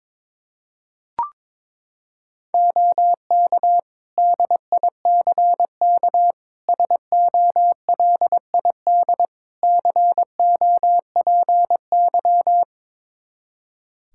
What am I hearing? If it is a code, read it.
Morse code, heard as a keyed tone, about 22 wpm, 700 Hz. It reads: OK DICK SOLID COPY
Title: Practice QSO - Exchange 3 - More Info - 22wpm